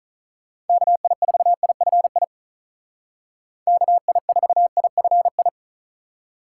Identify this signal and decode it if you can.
Morse code, heard as a keyed tone, about 44 wpm, 700 Hz. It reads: KI4IFI KI4IFI
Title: Speed Racing - USA 2x3 Call Signs 35wpm